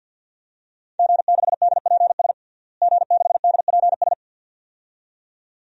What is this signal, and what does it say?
Morse code, heard as a keyed tone, about 47 wpm, 700 Hz. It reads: G6DPS G6DPS
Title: DXCC-100 Call Signs - 5 Characters - Repeated 2x - 50wpm